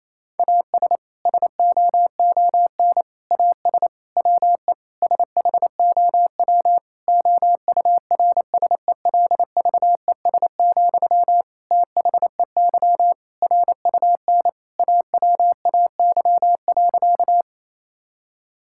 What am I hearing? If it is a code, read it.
Morse code, heard as a keyed tone, about 28 wpm, 700 Hz. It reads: AS SOON AS WE SHOW OURSELVES, THEY RUN AWAY.
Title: Sentences from Top 1600 Words 28wpm